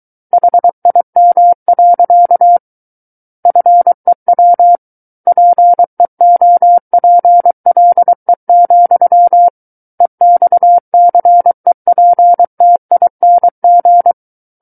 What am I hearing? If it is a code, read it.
Morse code, heard as a keyed tone, about 23 wpm, 700 Hz. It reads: HIM. FEW PEOPLE, EXCEPTING
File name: Book One / 14-Chapter 14 - 23 WPM